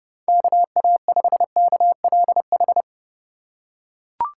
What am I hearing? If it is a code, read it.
Morse code, heard as a keyed tone, about 30 wpm, 700 Hz. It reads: KA5KLH
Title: USA Call Signs - 4x Character Spacing - 30wpm